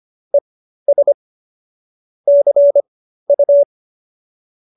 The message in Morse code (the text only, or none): ES CU